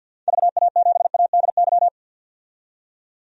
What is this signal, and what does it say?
Morse code, heard as a keyed tone, about 49 wpm, 700 Hz. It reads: VA7ADY
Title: DXCC-100 Call Signs - 6 Characters - 50wpm